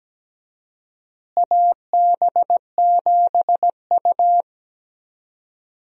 Morse code, heard as a keyed tone, about 17 wpm, 700 Hz. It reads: AB7U
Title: USA Call Signs - 4 Characters - Repeated 3x - 17wpm